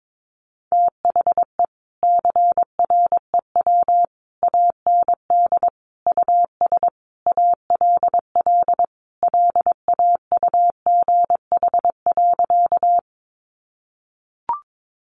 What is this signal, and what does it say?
Morse code, heard as a keyed tone, about 22 wpm, 700 Hz. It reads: THE CREW AND US ALL LAUGH.
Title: Sentences from Top 1500 Words 22wpm